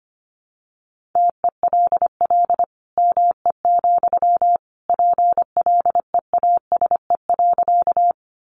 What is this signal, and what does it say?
Morse code, heard as a keyed tone, about 25 wpm, 700 Hz. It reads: TELL ME, PLEASE.
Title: Sentences from Top 1500 Words 25wpm